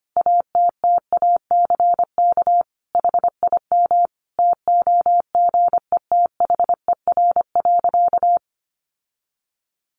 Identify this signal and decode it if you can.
Morse code, heard as a keyed tone, about 25 wpm, 700 Hz. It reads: ATTACK HIM TOGETHER.
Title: Sentences from Top 1500 Words 25wpm